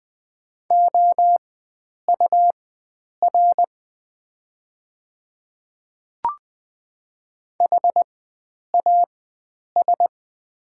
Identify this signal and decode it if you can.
Morse code, heard as a keyed tone, about 20 wpm, spaced as for 9 wpm, 690 Hz. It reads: OUR HAS
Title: 4x Character Spacing - Top 300 Words - 20wpm